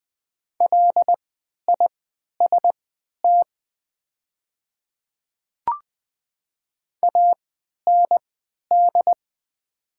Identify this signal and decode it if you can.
Morse code, heard as a keyed tone, about 20 wpm, spaced as for 11 wpm, 700 Hz. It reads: LIST AND
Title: 3x Character Spacing - Top 500 Words - 20wpm